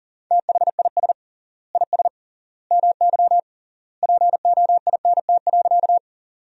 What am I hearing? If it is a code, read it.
Morse code, heard as a keyed tone, about 40 wpm, 700 Hz. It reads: THIS IS MY POINT.
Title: 3x Word-Spacing - Sentences from Top 200 Words - 40wpm